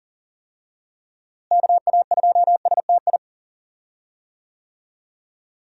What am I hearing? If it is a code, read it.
Morse code, heard as a keyed tone, about 37 wpm, 700 Hz. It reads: KA2STI
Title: USA Call Signs - 6 Characters - Repeated 2x - 40wpm